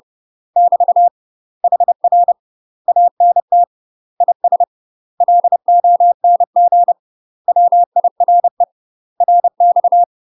Morse code, heard as a keyed tone, about 30 wpm, 700 Hz. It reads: <BT> HR ANT IS LONG WIRE RX
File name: UK MASTER QSOs TRAINING / 30WPM / UK MASTER QSOs 0005